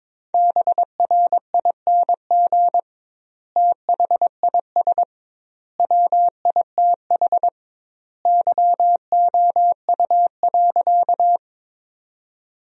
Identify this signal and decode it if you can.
Morse code, heard as a keyed tone, about 22 wpm, 700 Hz. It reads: BRING THIS WITH YOU.
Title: 2x Word-Spacing - Sentences from Top 500 Words - 22wpm